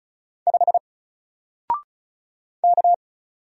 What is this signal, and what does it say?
Morse code, heard as a keyed tone, about 36 wpm, 700 Hz. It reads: 5 K
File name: Single-Letter-Number-35wpm